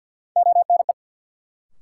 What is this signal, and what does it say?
Morse code, heard as a keyed tone, about 53 wpm, 700 Hz. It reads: ONE
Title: Sentences from Top 100 Words - New Word Drill - 50wpm